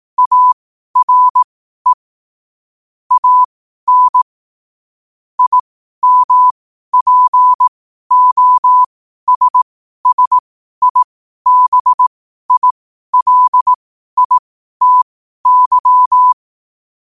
Morse code, heard as a keyed tone, about 18 wpm, spaced as for 12 wpm, 1000 Hz. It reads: ARE AN IMPOSSIBILITY
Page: New quotes every day in morse code at 12 Words per minute.